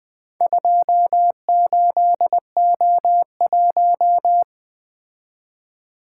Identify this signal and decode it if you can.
Morse code, heard as a keyed tone, about 24 wpm, 700 Hz. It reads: 28O1
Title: Speed-Racing - Four Letter-Number Combinations - 20wpm